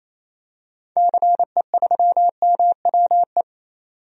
Morse code, heard as a keyed tone, about 28 wpm, 700 Hz. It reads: CE3MWE
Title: DXCC-100 Call Signs - 6 Characters - 28wpm